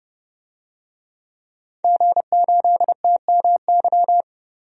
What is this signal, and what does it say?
Morse code, heard as a keyed tone, about 30 wpm, 700 Hz. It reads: G8TMY
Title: DXCC-100 Call Signs - 5 Characters - 30wpm